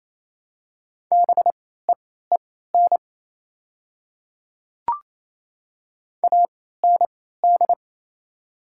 Morse code, heard as a keyed tone, about 27 wpm, spaced as for 16 wpm, 690 Hz. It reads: BEEN AND
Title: 3x Character Spacing - Top 100 Words - 28wpm